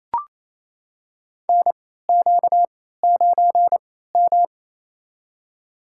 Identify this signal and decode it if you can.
Morse code, heard as a keyed tone, about 28 wpm, spaced as for 16 wpm, 700 Hz. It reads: NQ9M
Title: USA Call Signs - 4 Characters - 3x Character Spacing - 28wpm